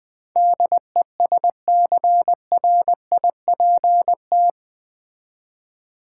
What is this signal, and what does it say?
Morse code, heard as a keyed tone, about 20 wpm, 700 Hz. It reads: DESCRIPT
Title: Build A Word - 4 Letters At A Time - 11-Letter Words - 20wpm